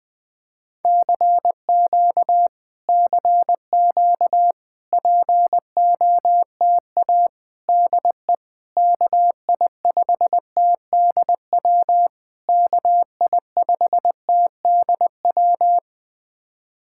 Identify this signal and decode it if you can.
Morse code, heard as a keyed tone, about 21 wpm, 700 Hz. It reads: CQ CQ POTA DE KI5TDW KI5TDW
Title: POTA QSO Practice - 20wpm